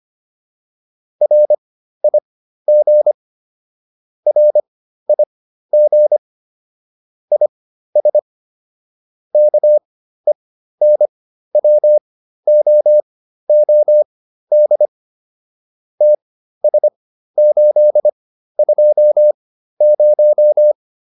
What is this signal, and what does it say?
Morse code, heard as a keyed tone, about 25 wpm, spaced as for 13 wpm, 600 Hz. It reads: RIG RIG IS KENWOOD TS820
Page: intermediate-practice-files